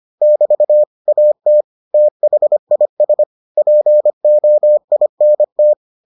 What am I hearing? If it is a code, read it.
Morse code, heard as a keyed tone, about 25 wpm, 600 Hz. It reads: <BT> AT THIS POINT